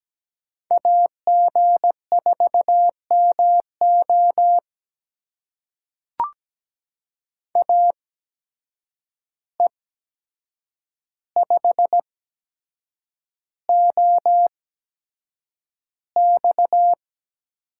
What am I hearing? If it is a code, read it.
Morse code, heard as a keyed tone, about 17 wpm, spaced as for 7 wpm, 700 Hz. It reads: AG4MO AE5OX
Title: USA Call Signs - 8x Character Spacing - 17wpm